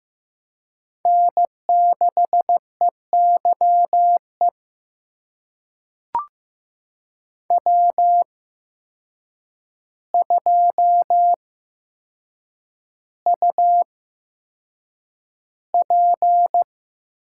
Morse code, heard as a keyed tone, about 15 wpm, spaced as for 7 wpm, 700 Hz. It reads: N6EYE W2UP
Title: USA Call Signs - 5 Characters - 8x Character Spacing - 15wpm